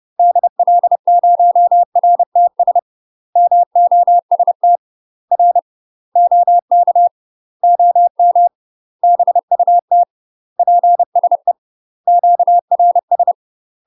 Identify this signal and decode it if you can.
Morse code, heard as a keyed tone, about 30 wpm, 700 Hz. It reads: DL0RTS MOST R OK OM BUT PSE QRS
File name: UK MASTER QSOs TRAINING / 30WPM / UK MASTER QSOs 0002